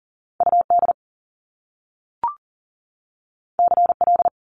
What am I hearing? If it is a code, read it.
Morse code, heard as a keyed tone, about 39 wpm, 700 Hz. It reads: UD CL
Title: Top Two Letter Combinations in Words 40wpm